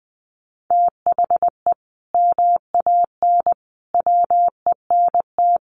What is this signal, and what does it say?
Morse code, heard as a keyed tone, about 20 wpm, 700 Hz. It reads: THE MAN WENT
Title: Sentences from Top 1400 Words 20wpm